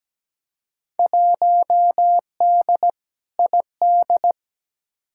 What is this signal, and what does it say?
Morse code, heard as a keyed tone, about 17 wpm, 700 Hz. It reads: 1D ID
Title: Field Day Exchanges - 17wpm